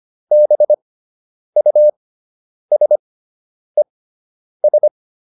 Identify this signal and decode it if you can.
Morse code, heard as a keyed tone, about 25 wpm, spaced as for 9 wpm, 600 Hz. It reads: BUSES